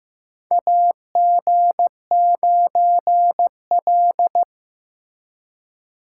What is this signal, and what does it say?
Morse code, heard as a keyed tone, about 15 wpm, 700 Hz. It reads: AG9L
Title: USA Call Signs - 4 Characters - 2x Character Spacing - 15wpm